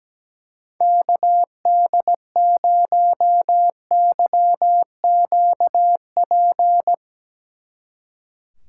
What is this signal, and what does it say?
Morse code, heard as a keyed tone, about 18 wpm, 700 Hz. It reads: KD0YQP
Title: Speed Racing - USA 2x3 Call Signs 17wpm